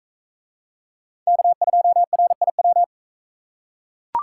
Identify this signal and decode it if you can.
Morse code, heard as a keyed tone, about 42 wpm, 700 Hz. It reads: K2RIW